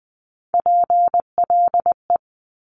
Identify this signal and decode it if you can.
Morse code, heard as a keyed tone, about 20 wpm, 700 Hz. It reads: PLE
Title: Top Three Letter Combinations in Words 20wpm